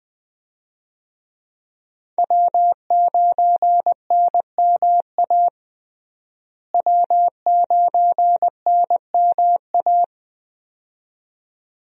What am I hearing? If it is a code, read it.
Morse code, heard as a keyed tone, about 20 wpm, 700 Hz. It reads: W9NMA W9NMA
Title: USA Call Signs - 5 Characters - Repeated 2x - 20wpm